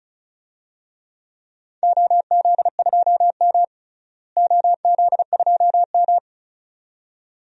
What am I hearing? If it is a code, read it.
Morse code, heard as a keyed tone, about 36 wpm, 700 Hz. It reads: OZ2M OZ2M
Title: DXCC-100 Call Signs - 4 Characters - Repeated 2x - 35wpm